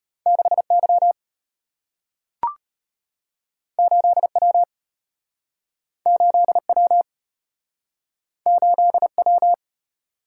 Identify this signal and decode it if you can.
Morse code, heard as a keyed tone, about 34 wpm, 700 Hz. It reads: 6Y 8W 8W 8W